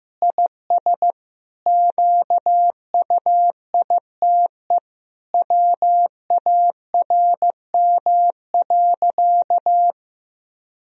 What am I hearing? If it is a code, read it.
Morse code, heard as a keyed tone, about 15 wpm, 700 Hz. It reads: IS QUITE WARM.
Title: Sentences from Top 1000 Words - Review Part 4 - 15wpm